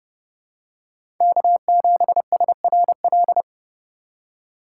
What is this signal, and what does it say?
Morse code, heard as a keyed tone, about 30 wpm, 700 Hz. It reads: K7SRL
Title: USA Call Signs - 5 Characters - 2x Character Spacing - 30wpm